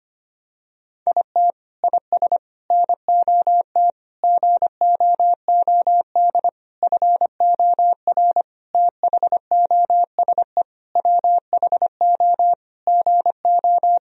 Morse code, heard as a keyed tone, about 25 wpm, 700 Hz. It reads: IT IS NOT GOOD FOR THOSE WHO GO
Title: Sentences from Top 1000 Words - Review Part 2 - 25wpm